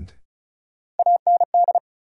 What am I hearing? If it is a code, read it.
Morse code, heard as a keyed tone, about 36 wpm, 700 Hz. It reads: AND
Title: Sets of 2 Words - New Word Drill - 35wpm